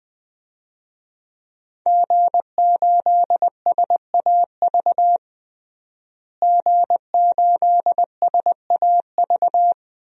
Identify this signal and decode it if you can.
Morse code, heard as a keyed tone, about 20 wpm, 700 Hz. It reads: G8SAV G8SAV
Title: DXCC-100 Call Signs - 5 Characters - Repeated 2x - 20wpm